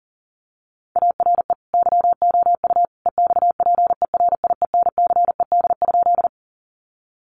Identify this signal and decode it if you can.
Morse code, heard as a keyed tone, about 40 wpm, 700 Hz. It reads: ARE YOU EXPERIENCED?
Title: Sentences from Top 1100 Words 40wpm